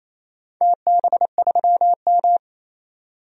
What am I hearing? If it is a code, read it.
Morse code, heard as a keyed tone, about 28 wpm, 700 Hz. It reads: TB3M
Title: DXCC-100 Call Signs - 4 Characters - 28wpm